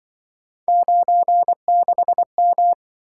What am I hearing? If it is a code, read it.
Morse code, heard as a keyed tone, about 25 wpm, 700 Hz. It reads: 96M
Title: Speed-Racing - Three Letter-Number Combinations - 20wpm